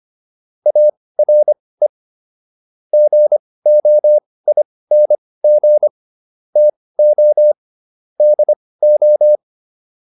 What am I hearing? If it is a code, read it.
Morse code, heard as a keyed tone, about 25 wpm, spaced as for 18 wpm, 600 Hz. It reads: ARE GOING TO DO